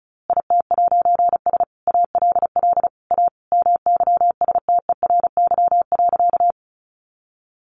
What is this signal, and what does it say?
Morse code, heard as a keyed tone, about 34 wpm, 700 Hz. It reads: IT'S ALL A MYSTERY.
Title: A Sound Of Thunder 35wpm